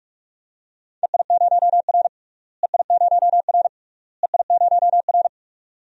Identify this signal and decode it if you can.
Morse code, heard as a keyed tone, about 46 wpm, 700 Hz. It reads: EI0R EI0R EI0R
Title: DXCC-100 Call Signs - 4 Characters - Repeated 3x - 45wpm